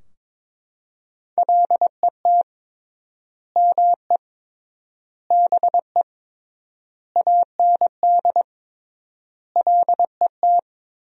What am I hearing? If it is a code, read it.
Morse code, heard as a keyed tone, about 22 wpm, 700 Hz. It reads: LET ME BE AND LET
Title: 3x Word-Spacing - Sentences from Top 300 Words - 22wpm